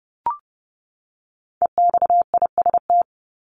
Morse code, heard as a keyed tone, about 30 wpm, 700 Hz. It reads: EXIST
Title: Top 751-1000 Words 30wpm